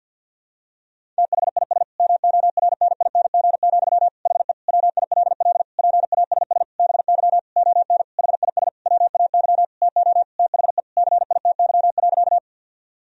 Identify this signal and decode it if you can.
Morse code, heard as a keyed tone, about 52 wpm, 700 Hz. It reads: THIS MORNING, HE WILL PASS BY ON HIS WAY TO THE CITY.
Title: Sentences from Top 1000 Words - Review Part 1 - 50wpm